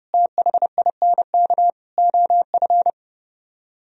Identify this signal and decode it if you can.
Morse code, heard as a keyed tone, about 36 wpm, 700 Hz. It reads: THINK OF
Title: Speed-Racing - Sets of 5 Words 30wpm